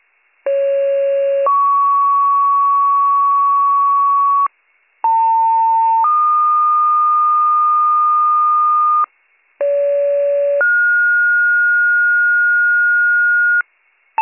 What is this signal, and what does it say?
Two-tone page, heard as a one-second tone, then a three-second tone, about 550 then 1120 Hz, about 900 then 1220 Hz, about 570 then 1430 Hz. It is heard from 0.5 to 13.6 s.